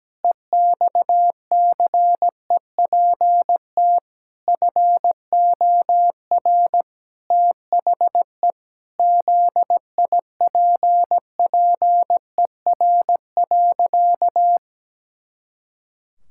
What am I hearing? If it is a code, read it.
Morse code, heard as a keyed tone, about 17 wpm, 700 Hz. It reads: EXCEPT FOR THE ZIPPER.
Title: Pangram Challenge v2 - 17wpm